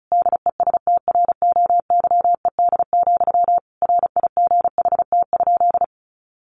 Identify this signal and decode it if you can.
Morse code, heard as a keyed tone, about 34 wpm, 700 Hz. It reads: DESTROYED, RIGHT?
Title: A Sound Of Thunder 35wpm